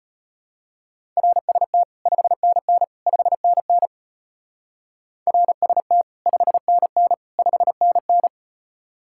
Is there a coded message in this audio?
Yes: Morse code from 1.2 to 8.3 s, heard as a keyed tone, about 32 wpm, 700 Hz. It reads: RST 5NN 5NN RST 5NN 5NN